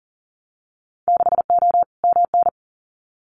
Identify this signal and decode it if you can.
Morse code, heard as a keyed tone, about 40 wpm, 700 Hz. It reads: 6O MN
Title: Winter Field Day Exchanges Encore 40wpm